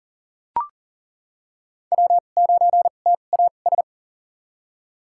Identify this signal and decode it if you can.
Morse code, heard as a keyed tone, about 40 wpm, spaced as for 29 wpm, 700 Hz. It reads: W9TAS
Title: USA Call Signs - 5 Characters - 2x Character Spacing - 40wpm